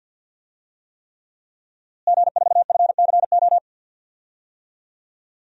Morse code, heard as a keyed tone, about 52 wpm, 700 Hz. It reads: G4FCO
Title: DXCC-100 Call Signs - 5 Characters - Repeated 3x - 50wpm